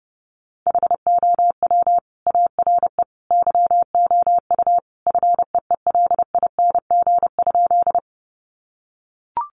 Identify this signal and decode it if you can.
Morse code, heard as a keyed tone, about 30 wpm, 700 Hz. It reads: HOW ARE YOU FEELING?
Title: Sentences from Top 1200 Words 30wpm